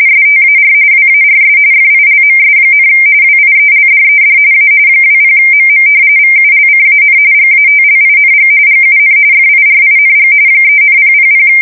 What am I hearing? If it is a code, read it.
RTTY, about 45.45 baud. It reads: STAR WARS FILMS, WIELDED A 20MM VULCAN GATLING GUN FIRING 3,000
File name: fsk170 / battleshipnj